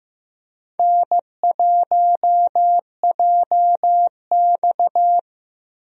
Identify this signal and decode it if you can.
Morse code, heard as a keyed tone, about 15 wpm, 700 Hz. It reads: N1JX
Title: USA Call Signs - 4 Characters - 15wpm